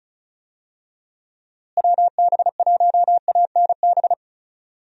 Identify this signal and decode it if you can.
Morse code, heard as a keyed tone, about 36 wpm, 700 Hz. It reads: WB1ANB
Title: USA Call Signs - 6 Characters - Repeated 3x - 35wpm